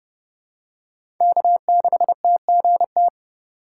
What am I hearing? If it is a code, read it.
Morse code, heard as a keyed tone, about 32 wpm, 700 Hz. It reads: K6TGT
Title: USA Call Signs - 5 Characters - 2x Character Spacing - 30wpm